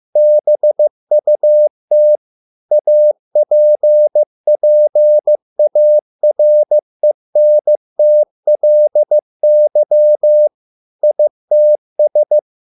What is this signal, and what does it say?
Morse code, heard as a keyed tone, about 15 wpm, 600 Hz. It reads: BUT APPARENTLY ITS